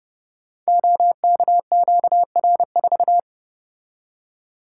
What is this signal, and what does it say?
Morse code, heard as a keyed tone, about 30 wpm, 700 Hz. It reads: OKQR4
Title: Code Groups - 5 Letter:Numbers - 30wpm